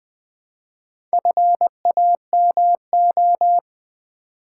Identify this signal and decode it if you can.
Morse code, heard as a keyed tone, about 21 wpm, 700 Hz. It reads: FAMO
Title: Build A Word - 2 Letters At A Time - 6-Letter Words - 20wpm